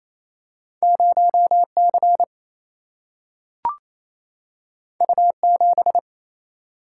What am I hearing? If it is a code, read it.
Morse code, heard as a keyed tone, about 28 wpm, 700 Hz. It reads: 0C U7
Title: Single-Letter Single-Number 28wpm